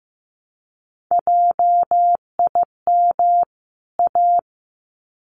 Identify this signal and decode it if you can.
Morse code, heard as a keyed tone, about 15 wpm, 700 Hz. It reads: JIM A
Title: CWT Exchange 15wpm